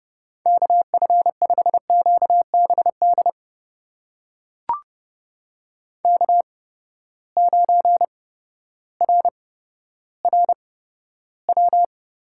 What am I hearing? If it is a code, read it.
Morse code, heard as a keyed tone, about 30 wpm, spaced as for 14 wpm, 700 Hz. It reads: KF5QBD K9RRW
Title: USA Call Signs - 8x Character Spacing - 30wpm